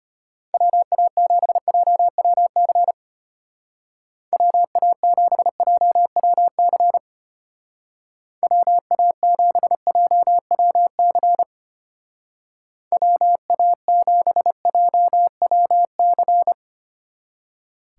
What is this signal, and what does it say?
Morse code, heard as a keyed tone, about 33 wpm, 700 Hz. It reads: WA7JWC WA7JWC WA7JWC WA7JWC